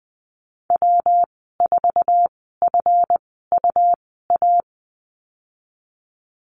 Morse code, heard as a keyed tone, about 19 wpm, spaced as for 14 wpm, 700 Hz. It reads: W4FUA
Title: 147 Lesson 24g - Letter y - Call Signs 20wpm